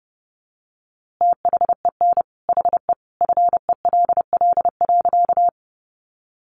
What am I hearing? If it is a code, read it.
Morse code, heard as a keyed tone, about 30 wpm, 700 Hz. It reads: THEN HE FELL.
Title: A Sound Of Thunder 30wpm